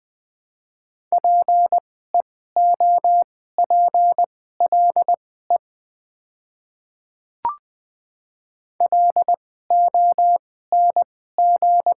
Morse code, heard as a keyed tone, about 20 wpm, spaced as for 14 wpm, 690 Hz. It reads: PEOPLE LONG
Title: 2x Character Spacing - Top 100 Words - 20wpm